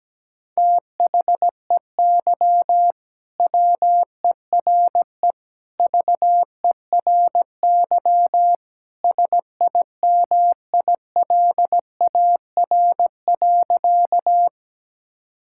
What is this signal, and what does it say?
Morse code, heard as a keyed tone, about 17 wpm, 700 Hz. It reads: THEY WERE VERY SIMILAR.
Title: Sentences from Top 1000 Words - Review Part 5 - 17wpm